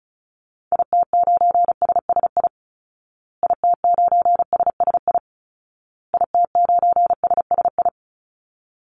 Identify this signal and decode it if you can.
Morse code, heard as a keyed tone, about 34 wpm, 700 Hz. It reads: IT9SSI IT9SSI IT9SSI
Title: Call Signs Repeated 2x - 35wpm